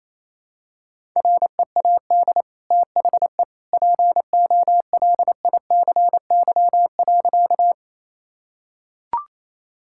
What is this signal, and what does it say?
Morse code, heard as a keyed tone, about 27 wpm, 700 Hz. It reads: READ THE POLICY.
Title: Sentences from Top 1000 Words - Review Part 5 - 28wpm